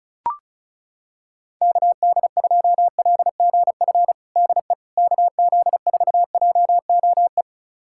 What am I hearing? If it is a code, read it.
Morse code, heard as a keyed tone, about 36 wpm, 700 Hz. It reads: KD2LGF DE KZ4JOE
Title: USA Callsign DE Callsign - 6 Characters - 35wpm